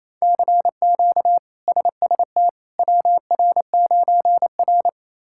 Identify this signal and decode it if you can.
Morse code, heard as a keyed tone, about 28 wpm, 700 Hz. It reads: CQ SST WR9R
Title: K1USN SST QSO Practice - 28wpm